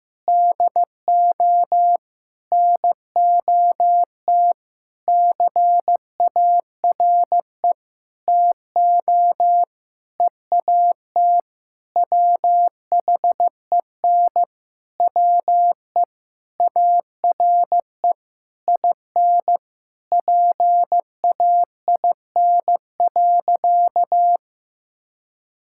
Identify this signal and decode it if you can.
Morse code, heard as a keyed tone, about 15 wpm, 700 Hz. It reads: DO NOT CARE TO EAT WHEN WE ARE IN PAIN.
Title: Sentences from Top 1600 Words 15wpm